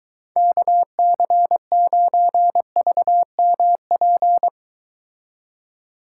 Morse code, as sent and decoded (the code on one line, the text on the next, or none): -.- -.-. ----. ...- -- .--.
KC9VMP